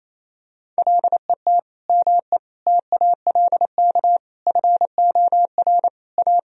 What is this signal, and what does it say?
Morse code, heard as a keyed tone, about 28 wpm, 700 Hz. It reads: LET ME TALK FOR A
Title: Sentences from Top 1000 Words - Review Part 1 - 28wpm